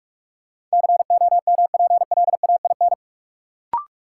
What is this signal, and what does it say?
Morse code, heard as a keyed tone, about 45 wpm, 700 Hz. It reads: COMPLAIN
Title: Sentences from Top 1300 Words - New Word Drill - 45wpm